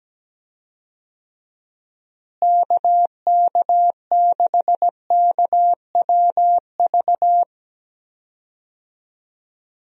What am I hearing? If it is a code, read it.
Morse code, heard as a keyed tone, about 17 wpm, 700 Hz. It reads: KK6KWV
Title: USA Call Signs - 6 Characters - Repeated 3x - 17wpm